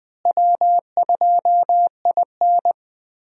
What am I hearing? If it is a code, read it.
Morse code, heard as a keyed tone, about 20 wpm, 700 Hz. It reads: W2IN
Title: USA Call Signs - 4 Characters - Repeated 2x - 20wpm